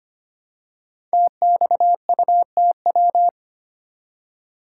Morse code, heard as a keyed tone, about 25 wpm, 700 Hz. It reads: TXUTW
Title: Code Groups - 5 Letters - 25wpm